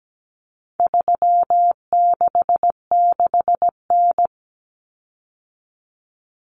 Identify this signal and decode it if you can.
Morse code, heard as a keyed tone, about 17 wpm, 700 Hz. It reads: 366N
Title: Common Cut Numbers - 4-Digit 17wpm